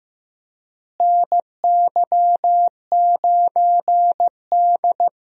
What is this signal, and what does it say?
Morse code, heard as a keyed tone, about 15 wpm, 700 Hz. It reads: NY9D
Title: USA Call Signs - 4 Characters - Repeated 2x - 15wpm